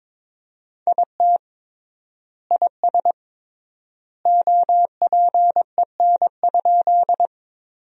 Morse code, heard as a keyed tone, about 22 wpm, 700 Hz. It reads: IT IS OPEN?
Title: 3x Word-Spacing - Sentences from Top 300 Words - 22wpm